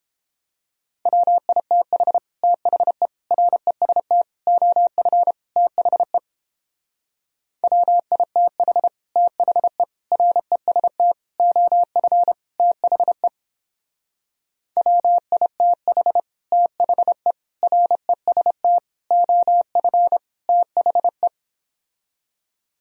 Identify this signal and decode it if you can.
Morse code, heard as a keyed tone, about 29 wpm, 700 Hz. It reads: WITH THE REST OF THE WITH THE REST OF THE WITH THE REST OF THE